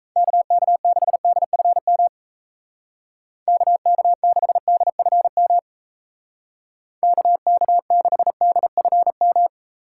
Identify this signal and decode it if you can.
Morse code, heard as a keyed tone, about 35 wpm, 700 Hz. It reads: KK6DFM KK6DFM KK6DFM
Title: Speed Racing - USA 2x3 Call Signs 28wpm